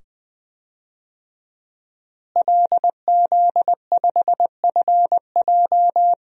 Morse code, heard as a keyed tone, about 20 wpm, 700 Hz. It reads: LZ5FJ
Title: DXCC-100 Call Signs - 5 Characters - Repeated 2x - 20wpm